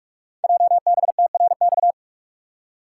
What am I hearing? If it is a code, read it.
Morse code, heard as a keyed tone, about 44 wpm, 700 Hz. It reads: JBTRX
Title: Code Groups - 5 Letters - 45wpm